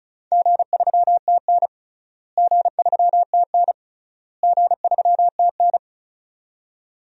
Morse code, heard as a keyed tone, about 36 wpm, 700 Hz. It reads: G3TN G3TN G3TN
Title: DXCC-100 Call Signs - 4 Characters - Repeated 3x - 35wpm